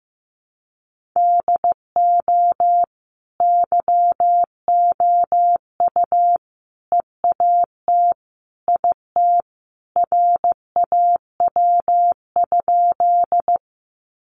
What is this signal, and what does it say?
Morse code, heard as a keyed tone, about 15 wpm, 700 Hz. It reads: DO YOU EAT IT RAW?
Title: Sentences from Top 1500 Words 15wpm